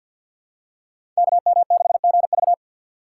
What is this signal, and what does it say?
Morse code, heard as a keyed tone, about 53 wpm, 700 Hz. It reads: KM6GV
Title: USA Call Signs - 5 Characters - 3x Character Spacing - 50wpm